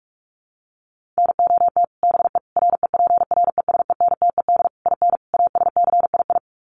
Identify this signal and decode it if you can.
Morse code, heard as a keyed tone, about 44 wpm, 700 Hz. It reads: NOT BE REPRESENTED IN ASCII
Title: Sets of 5 Words 45wpm